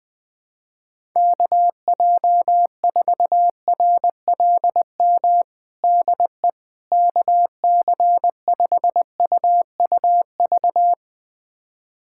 Morse code, heard as a keyed tone, about 20 wpm, 700 Hz. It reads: KJ4RLM DE KC5UUV
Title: USA Callsign DE Callsign - 6 Characters - 20wpm